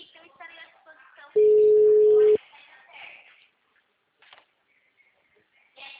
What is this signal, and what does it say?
Ringback tone (Europe, 425 Hz), with speech.
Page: Itatiba 3